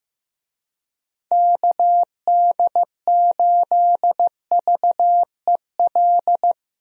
Morse code, heard as a keyed tone, about 15 wpm, 700 Hz. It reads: KD8VEL
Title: USA Call Signs - 6 Characters - Repeated 3x - 15wpm